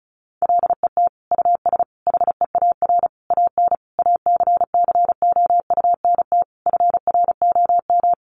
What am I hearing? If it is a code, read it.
Morse code, heard as a keyed tone, about 34 wpm, 700 Hz. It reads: LET US HEAR AN ACCOUNT FROM
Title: Sentences from Top 900 Words 35wpm